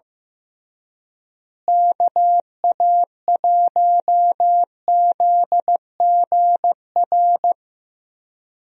Morse code, heard as a keyed tone, about 15 wpm, 700 Hz. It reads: KA1ZGR
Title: USA Call Signs - 6 Characters - Repeated 3x - 15wpm